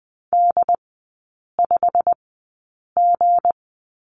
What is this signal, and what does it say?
Morse code, heard as a keyed tone, about 20 wpm, spaced as for 8 wpm, 700 Hz. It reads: D5G
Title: 178 Lesson 29e - Slash - Three Characters 20wpm